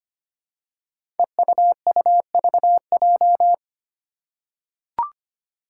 Morse code, heard as a keyed tone, about 25 wpm, 700 Hz. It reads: EUUVJ
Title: Five Challenging Letter-Number Combinations 25wpm